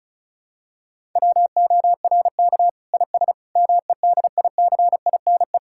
Morse code, heard as a keyed tone, about 36 wpm, 700 Hz. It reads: WORK IS MEDICINE
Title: Sentences from Top 1000 Words - Review Part 6 - 35wpm